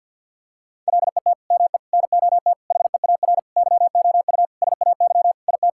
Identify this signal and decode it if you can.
Morse code, heard as a keyed tone, about 50 wpm, 700 Hz. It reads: LET ME NOT HEAR YOU SAY IT
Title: Sentences from Top 1600 Words 50wpm